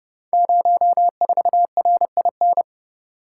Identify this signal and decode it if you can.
Morse code, heard as a keyed tone, about 30 wpm, 700 Hz. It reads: 04RIN
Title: Code Groups - 5 Letter:Numbers - 30wpm